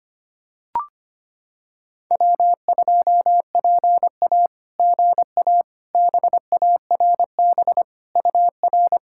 Morse code, heard as a keyed tone, about 25 wpm, 700 Hz. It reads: W2PA GA BARB UR
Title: SOTA QSO Practice 25wpm